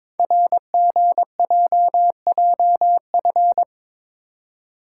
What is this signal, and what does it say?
Morse code, heard as a keyed tone, about 23 wpm, 700 Hz. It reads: RGJJF
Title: Code Groups - 5 Letters - 22wpm